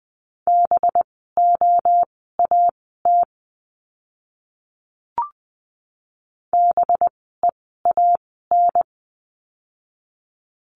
Morse code, heard as a keyed tone, about 20 wpm, spaced as for 14 wpm, 700 Hz. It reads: BOAT BEAN
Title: 185 Lesson 30f - Letter b - Words 20wpm